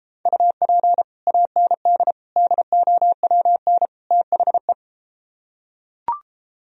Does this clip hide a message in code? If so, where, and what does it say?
Yes: Morse code from 0.3 to 4.7 s, heard as a keyed tone, about 33 wpm, 700 Hz. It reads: UP AND DOWN THE